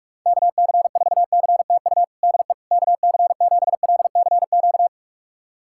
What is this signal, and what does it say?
Morse code, heard as a keyed tone, about 46 wpm, 700 Hz. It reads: KC4CTU DE KC7LCQ
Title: USA Callsign DE Callsign - 6 Characters - 45wpm